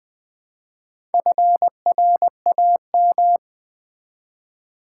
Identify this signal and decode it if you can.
Morse code, heard as a keyed tone, about 20 wpm, 700 Hz. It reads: FRAM
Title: Build A Word - 1 Letter At A Time - 3 to 5-Letter Words - 20wpm